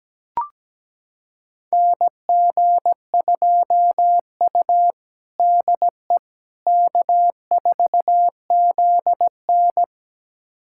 Morse code, heard as a keyed tone, about 17 wpm, 700 Hz. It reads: NG2U DE K4ZN
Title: USA Callsign DE Callsign - 4 Characters - 17wpm